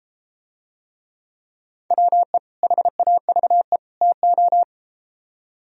Morse code, heard as a keyed tone, about 33 wpm, 700 Hz. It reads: WE HAVE TO